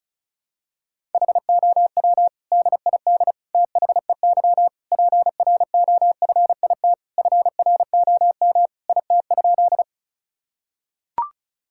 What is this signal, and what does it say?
Morse code, heard as a keyed tone, about 36 wpm, 700 Hz. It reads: HOW DID THEY PROFIT FROM IT?
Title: Sentences from Top 1000 Words - Review Part 5 - 35wpm